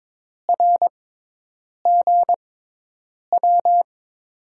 Morse code, heard as a keyed tone, about 22 wpm, spaced as for 8 wpm, 700 Hz. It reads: RGW
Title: USA Call Signs - 5 Characters - 6x Character Spacing - 22wpm